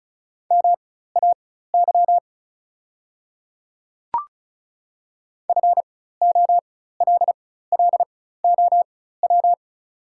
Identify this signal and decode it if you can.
Morse code, heard as a keyed tone, about 36 wpm, spaced as for 16 wpm, 700 Hz. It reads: MAY FOLLOW
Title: 4x Character Spacing - Top 300 Words - 35wpm